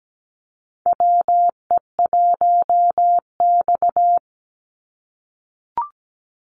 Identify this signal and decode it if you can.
Morse code, heard as a keyed tone, about 17 wpm, 700 Hz. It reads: WE1X
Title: USA Call Signs 17wpm